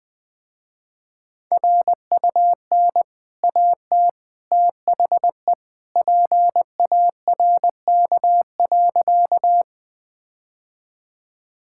Morse code, heard as a keyed tone, about 20 wpm, 700 Hz. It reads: RUN AT THE PARK.
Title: Easy Sentences with Hints - 20wpm